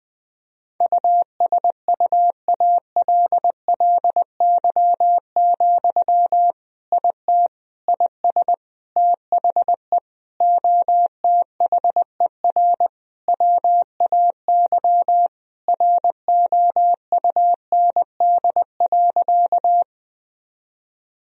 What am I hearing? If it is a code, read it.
Morse code, heard as a keyed tone, about 20 wpm, 700 Hz. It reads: USUALLY, IT IS THE OTHER WAY ROUND.
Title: Sentences from Top 1000 Words - Review Part 3 - 20wpm